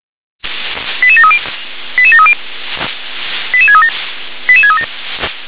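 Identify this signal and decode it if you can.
Five-tone selective call (ZVEI), heard as quick runs of five tones, tones of about 70 ms. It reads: 88730 88730 88637 88637
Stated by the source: ZVEI five-tone call 88084 received by SDR.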